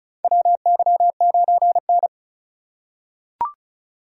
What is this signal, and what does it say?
Morse code, heard as a keyed tone, about 34 wpm, 700 Hz. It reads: WY9N
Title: USA Call Signs - 4 Characters - 3x Character Spacing - 35wpm